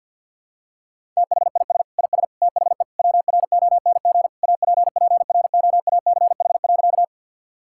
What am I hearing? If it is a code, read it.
Morse code, heard as a keyed tone, about 52 wpm, 700 Hz. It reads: THIS IS THE WRONG APPROACH.
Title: Sentences from Top 1000 Words - Review Part 6 - 50wpm